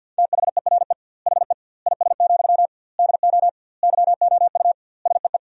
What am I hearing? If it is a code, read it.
Morse code, heard as a keyed tone, about 50 wpm, 700 Hz. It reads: THERE HE IS, DO YOU SEE
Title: 2x Word-Spacing - Sentences from Top 100 Words - 50wpm